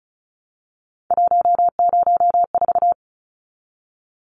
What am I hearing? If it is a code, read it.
Morse code, heard as a keyed tone, about 34 wpm, 700 Hz. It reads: 104
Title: Numbers - 3-Digit 35wpm